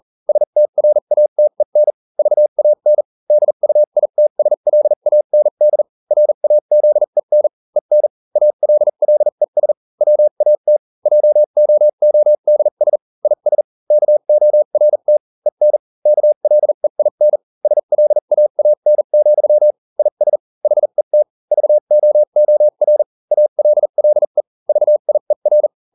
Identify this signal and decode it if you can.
Morse code, heard as a keyed tone, about 40 wpm, spaced as for 35 wpm, 600 Hz. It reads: STRATEN VAN DUITSLAND RAZEN EN ALLES WAT JOODS IS KORT EN KLEIN SLAAN, IS HET VOOR ALLE VIER